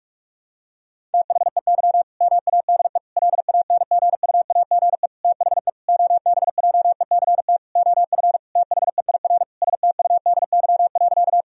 Morse code, heard as a keyed tone, about 46 wpm, 700 Hz. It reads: THEY MADE LANGUAGE THE OBJECT OF THEIR STUDY.
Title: Sentences from Top 1000 Words - Review Part 3 - 45wpm